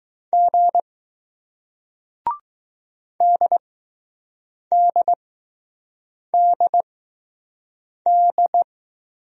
Morse code, heard as a keyed tone, about 21 wpm, spaced as for 6 wpm, 700 Hz. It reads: G DDDD